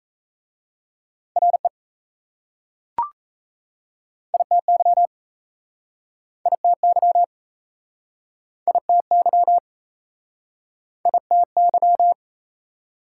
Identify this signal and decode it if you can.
Morse code, heard as a keyed tone, about 39 wpm, 700 Hz. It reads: RE ITY ITY ITY ITY